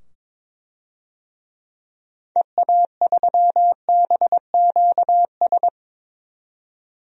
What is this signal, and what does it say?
Morse code, heard as a keyed tone, about 22 wpm, 700 Hz. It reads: EA3BQS
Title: DXCC-100 Call Signs - 6 Characters - 22wpm